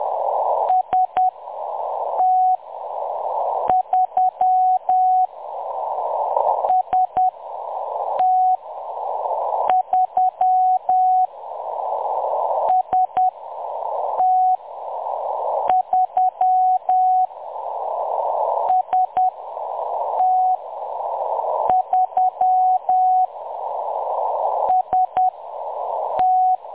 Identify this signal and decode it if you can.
Morse code, heard as a keyed tone, about 10 wpm, spaced as for 6 wpm, 740 Hz. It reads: ST3 ST3 ST3 ST3 ST
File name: M23-ST3- -0858utc-10184khz